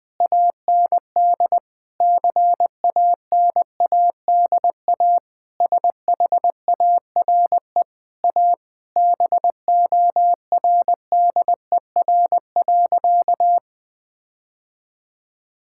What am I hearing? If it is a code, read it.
Morse code, heard as a keyed tone, about 20 wpm, 700 Hz. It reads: AND CANADA SHARE A BORDER.
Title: Sentences with 5-Word Phrases - Spoken Beginning - 20wpm